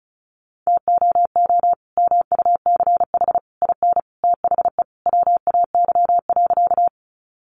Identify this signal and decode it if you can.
Morse code, heard as a keyed tone, about 34 wpm, 700 Hz. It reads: TOO MUCH IN THE WAY.
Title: Sentences from Top 200 Words 35wpm